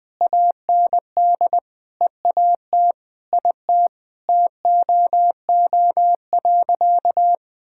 Morse code, heard as a keyed tone, about 20 wpm, 700 Hz. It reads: AND EAT IT TOO.
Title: Sentences with 8-Word Phrases - 20wpm